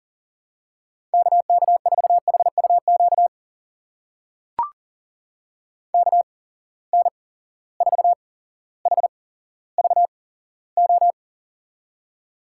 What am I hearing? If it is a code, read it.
Morse code, heard as a keyed tone, about 41 wpm, spaced as for 17 wpm, 700 Hz. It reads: KK4HUQ KN4HVO
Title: USA Call Signs - 6 Characters - 8x Character Spacing - 40wpm